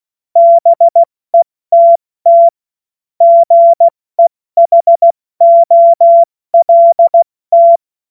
Morse code, heard as a keyed tone, about 16 wpm, spaced as for 14 wpm, 680 Hz. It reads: BETT GEHOLT
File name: Morsekurs/Witze